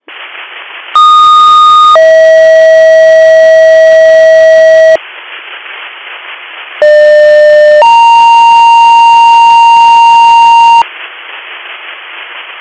Two-tone page, heard as a one-second tone, then a three-second tone, about 1230 then 640 Hz, about 600 then 930 Hz. It is heard from 1.0 to 10.8 s.